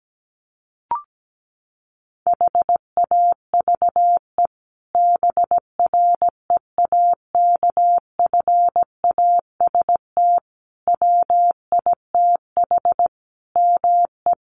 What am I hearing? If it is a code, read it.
Morse code, heard as a keyed tone, about 17 wpm, 700 Hz. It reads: HAVE BREAKFAST WITH ME
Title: Sentences from Top 1500 Words 17wpm